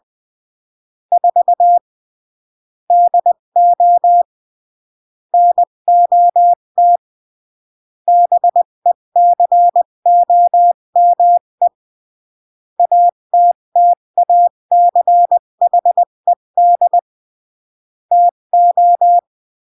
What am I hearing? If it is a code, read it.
Morse code, heard as a keyed tone, about 20 wpm, 700 Hz. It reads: 4 DO NOT BECOME ATTACHED TO
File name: CW / jodorowsky / 82 commandments 20-15